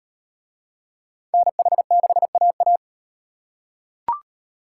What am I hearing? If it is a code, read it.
Morse code, heard as a keyed tone, about 38 wpm, 700 Hz. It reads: NH6AA